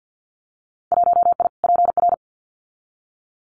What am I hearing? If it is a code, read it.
Morse code, heard as a keyed tone, about 50 wpm, 700 Hz. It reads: 1I PR
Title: Winter Field Day Exchanges Encore 50wpm